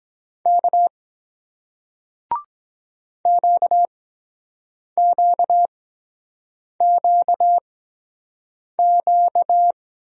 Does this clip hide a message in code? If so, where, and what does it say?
Yes: Morse code from 0.5 to 9.7 s, heard as a keyed tone, about 23 wpm, spaced as for 7 wpm, 700 Hz. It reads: K QQQQ